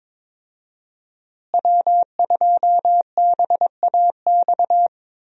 Morse code, heard as a keyed tone, about 22 wpm, 700 Hz. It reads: W2BAX
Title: USA Call Signs - 5 Characters - Repeated 2x - 22wpm